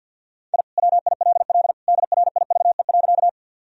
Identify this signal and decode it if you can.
Morse code, heard as a keyed tone, about 52 wpm, 700 Hz. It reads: I WILL DRIVE.
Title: Sentences from Top 1000 Words - Review Part 2 - 50wpm